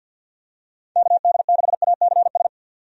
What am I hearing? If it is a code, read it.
Morse code, heard as a keyed tone, about 51 wpm, 700 Hz. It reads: KD6ACS
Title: USA Call Signs - 6 Characters - 50wpm